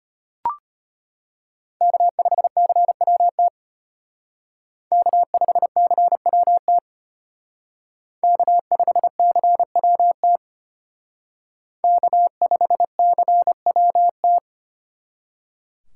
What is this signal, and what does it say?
Morse code, heard as a keyed tone, about 33 wpm, 700 Hz. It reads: K5CWT K5CWT K5CWT K5CWT